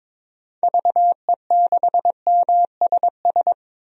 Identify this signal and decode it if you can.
Morse code, heard as a keyed tone, about 22 wpm, 700 Hz. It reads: VE6MSS
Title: DXCC-100 Call Signs - 6 Characters - 22wpm